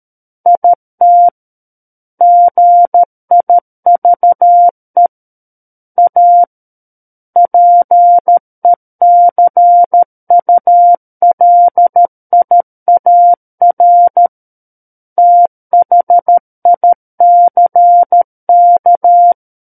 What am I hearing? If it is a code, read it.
Morse code, heard as a keyed tone, about 13 wpm, 700 Hz. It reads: IT GIVE A PECULIAR THICK
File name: Book One / 04-Chapter 4 - 13 WPM